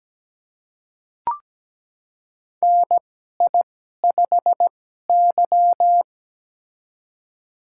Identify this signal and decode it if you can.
Morse code, heard as a keyed tone, about 17 wpm, spaced as for 12 wpm, 700 Hz. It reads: NI5Y
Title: USA Call Signs - 2x Character Spacing - 17wpm